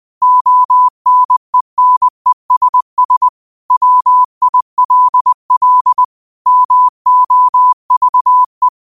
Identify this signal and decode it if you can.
Morse code, heard as a keyed tone, about 20 wpm, 1000 Hz. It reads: ONENESS WILL MOVE
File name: qotdfiles / 20wpm